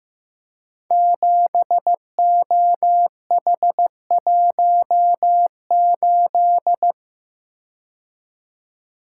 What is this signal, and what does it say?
Morse code, heard as a keyed tone, about 15 wpm, 700 Hz. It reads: 7OH18
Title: Code Groups - 5 Letter:Numbers - 15wpm